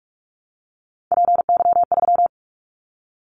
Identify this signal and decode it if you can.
Morse code, heard as a keyed tone, about 46 wpm, 700 Hz. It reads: PY3
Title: Callsign Prefixes for Top 50 DXCC - 45wpm